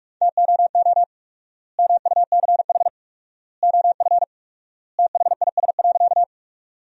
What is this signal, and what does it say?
Morse code, heard as a keyed tone, about 46 wpm, 700 Hz. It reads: TOO MUCH OF THIS.
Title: 4x Word-Spacing - Sentences from Top 200 Words - 45wpm